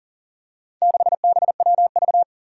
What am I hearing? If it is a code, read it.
Morse code, heard as a keyed tone, about 40 wpm, 700 Hz. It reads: 6BWV
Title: Four Challenging Letter-Number Combinations 40wpm